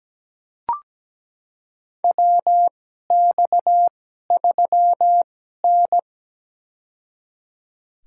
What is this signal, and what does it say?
Morse code, heard as a keyed tone, about 17 wpm, spaced as for 12 wpm, 700 Hz. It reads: WX3N
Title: USA Call Signs - 4 Characters - 2x Character Spacing - 17wpm